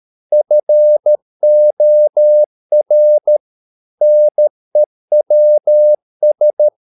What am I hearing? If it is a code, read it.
Morse code, heard as a keyed tone, about 13 wpm, 600 Hz. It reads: FOR NEWS